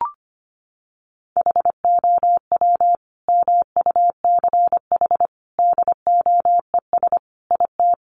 Morse code, heard as a keyed tone, about 25 wpm, 700 Hz. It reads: HOW MUCH DOES IT
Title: Sentences from Top 800 Words 25wpm